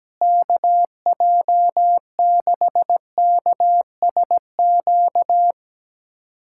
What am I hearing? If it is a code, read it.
Morse code, heard as a keyed tone, about 17 wpm, 700 Hz. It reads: KJ6KSQ
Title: USA Callsign DE Callsign 17wpm